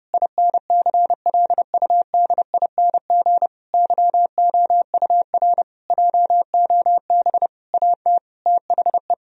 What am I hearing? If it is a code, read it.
Morse code, heard as a keyed tone, about 30 wpm, 700 Hz. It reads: INCLUDING YOUR JOB AT THE
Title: Sentences from Top 1000 Words - Review Part 4 - 30wpm